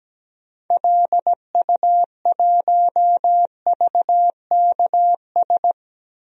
Morse code, heard as a keyed tone, about 17 wpm, 700 Hz. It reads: LU1VKS
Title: DXCC-100 Call Signs - 6 Characters - 17wpm